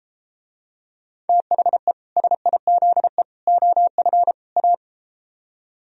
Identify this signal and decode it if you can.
Morse code, heard as a keyed tone, about 34 wpm, 700 Hz. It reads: THE SIZE OF A